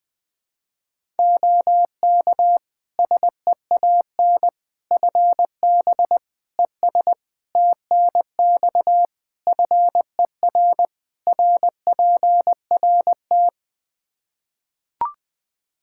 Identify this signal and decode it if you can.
Morse code, heard as a keyed tone, about 20 wpm, 700 Hz. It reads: OK SEAN FB ES TNX FER RPRT
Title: Practice QSO - Exchange 2 - Info - 20wpm